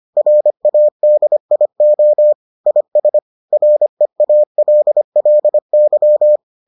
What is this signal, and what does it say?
Morse code, heard as a keyed tone, about 25 wpm, 600 Hz. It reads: RADIO IS REALLY